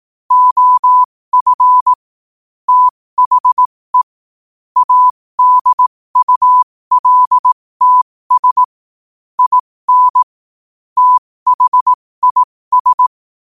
Morse code, heard as a keyed tone, about 18 wpm, spaced as for 15 wpm, 1000 Hz. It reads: OF THE ADULTS IN THIS
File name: qotdfiles / 15wpm